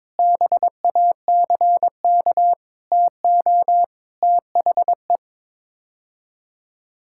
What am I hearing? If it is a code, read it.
Morse code, heard as a keyed tone, about 22 wpm, 700 Hz. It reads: BACK TO THE
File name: practice / 22wpm / Sets of 6 Words - 22wpm